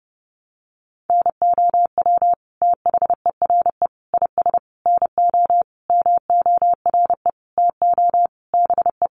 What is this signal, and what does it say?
Morse code, heard as a keyed tone, about 28 wpm, 700 Hz. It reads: NOW THERE IS NO MORE TO BE
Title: Sentences from Top 100 Words 30wpm